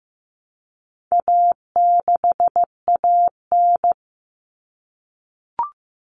Morse code, heard as a keyed tone, about 15 wpm, 700 Hz. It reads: A6AN
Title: Common Cut Numbers - 4-Digit 15wpm